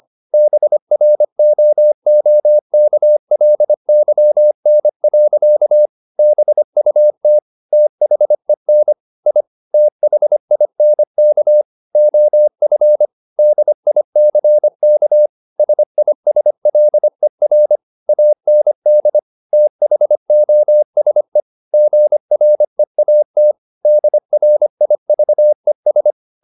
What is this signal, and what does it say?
Morse code, heard as a keyed tone, about 25 wpm, 600 Hz. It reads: BROOKLYN. BUT THEN I THINK OF DICK SISLER AND THOSE GREAT DRIVES
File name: the old man and the sea part 1 25wpm